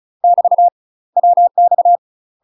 Morse code, heard as a keyed tone, about 36 wpm, 700 Hz. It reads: <BT> WX
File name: UK MASTER QSOs TRAINING / 35WPM / UK MASTER QSOs 0022